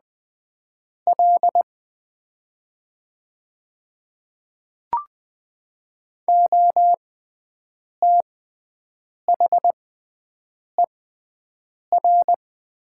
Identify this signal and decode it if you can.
Morse code, heard as a keyed tone, about 20 wpm, spaced as for 7 wpm, 700 Hz. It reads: L OTHER
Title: 6x Character Spacing - Top 200 Words - 20wpm